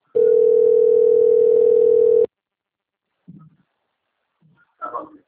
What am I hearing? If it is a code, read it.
Ringback tone (North America), 440 + 480 Hz.
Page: Atibaia - SP 1 (CP Cartao)